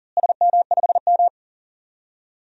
Morse code, heard as a keyed tone, about 40 wpm, 700 Hz. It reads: SM5M
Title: DXCC-100 Call Signs - 4 Characters - Repeated 3x - 40wpm